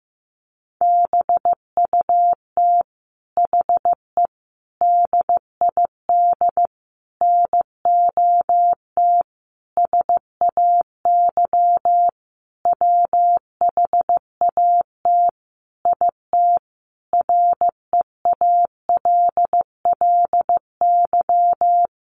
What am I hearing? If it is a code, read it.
Morse code, heard as a keyed tone, about 15 wpm, 700 Hz. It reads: BUT HE DID NOT SAY WHAT IT REALLY
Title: Sentences from Top 600 Words 15wpm